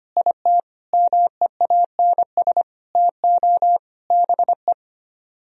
Morse code, heard as a keyed tone, about 31 wpm, 700 Hz. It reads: IT MEANS TO BE
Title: Speed-Racing - Sets of 5 Words 25wpm